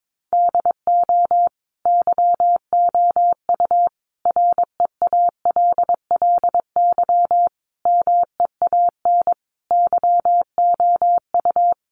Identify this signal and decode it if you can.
Morse code, heard as a keyed tone, about 22 wpm, 700 Hz. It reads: DO YOU REALLY MEAN YOU
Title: Sentences from Top 600 Words 22wpm